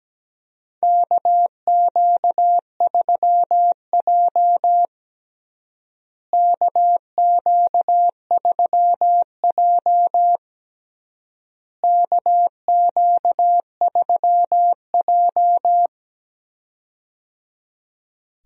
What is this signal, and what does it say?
Morse code, heard as a keyed tone, about 17 wpm, 700 Hz. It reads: KQ3J KQ3J KQ3J
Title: USA Call Signs - 4 Characters - Repeated 3x - 17wpm